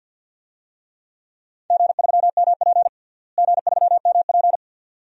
Morse code, heard as a keyed tone, about 52 wpm, 700 Hz. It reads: G3MP G3MP
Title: DXCC-100 Call Signs - 4 Characters - Repeated 2x - 50wpm